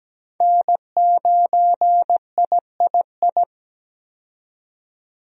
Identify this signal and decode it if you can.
Morse code, heard as a keyed tone, about 17 wpm, 700 Hz. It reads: N9III
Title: USA Call Signs - 5 Characters - 17wpm